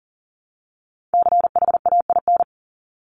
Morse code, heard as a keyed tone, about 40 wpm, 700 Hz. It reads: CHAIN
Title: Top 1251-1500 Words 40wpm